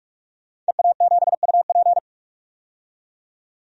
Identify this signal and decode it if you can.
Morse code, heard as a keyed tone, about 44 wpm, 700 Hz. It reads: EA7UP
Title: DXCC-100 Call Signs - 5 Characters - 45wpm